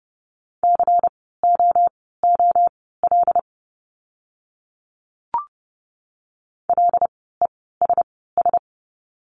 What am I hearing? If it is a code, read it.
Morse code, heard as a keyed tone, about 30 wpm, spaced as for 17 wpm, 700 Hz. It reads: COOL LESS
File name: Lesson 16f - Letter c - Words 30wpm